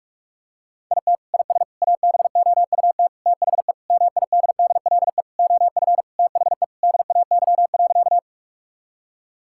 Morse code, heard as a keyed tone, about 44 wpm, 700 Hz. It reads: IT IS ABOUT THE MIDDLE OF THE DAY.
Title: Sentences from Top 1000 Words - Review Part 6 - 45wpm